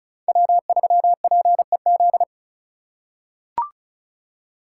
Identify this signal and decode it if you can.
Morse code, heard as a keyed tone, about 34 wpm, 700 Hz. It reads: W3PEZ
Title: USA Call Signs - 3x Character Spacing - 35wpm